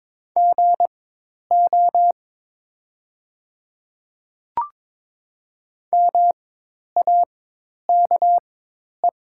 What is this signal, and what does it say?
Morse code, heard as a keyed tone, about 22 wpm, spaced as for 10 wpm, 700 Hz. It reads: GO MAKE
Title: 4x Character Spacing - Top 100 Words - 22wpm